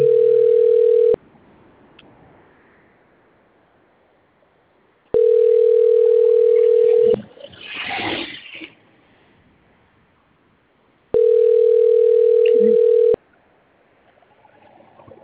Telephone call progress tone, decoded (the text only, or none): ringback tone